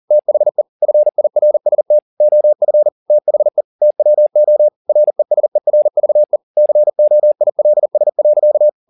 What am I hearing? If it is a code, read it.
Morse code, heard as a keyed tone, about 40 wpm, 600 Hz. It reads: THE FIRST OF THE TWO RESERVE COILS.